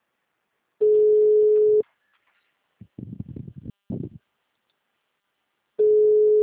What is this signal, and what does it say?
Ringback tone (Europe, 425 Hz).